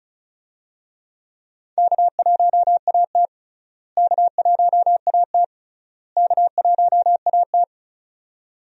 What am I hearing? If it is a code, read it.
Morse code, heard as a keyed tone, about 36 wpm, 700 Hz. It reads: K1AT K1AT K1AT
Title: USA Call Signs - 4 Characters - Repeated 3x - 35wpm